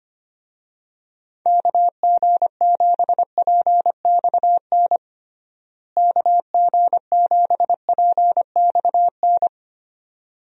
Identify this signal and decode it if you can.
Morse code, heard as a keyed tone, about 25 wpm, 700 Hz. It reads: KG7PXN KG7PXN
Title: USA Call Signs - 6 Characters - Repeated 2x - 25wpm